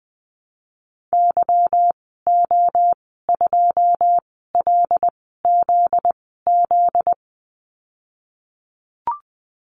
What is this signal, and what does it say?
Morse code, heard as a keyed tone, about 20 wpm, spaced as for 15 wpm, 700 Hz. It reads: YO2LZZ
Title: 245 Lesson 39g - Letter z - Call Signs 20wpm